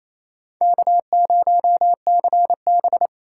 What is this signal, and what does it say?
Morse code, heard as a keyed tone, about 35 wpm, 700 Hz. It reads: K0CB
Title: Speed Racing - USA 1x2 Call Signs 28wpm